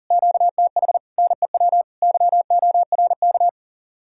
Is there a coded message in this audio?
Yes: Morse code from 0.1 to 3.5 s, heard as a keyed tone, about 51 wpm, 700 Hz. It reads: QTH NEW YORK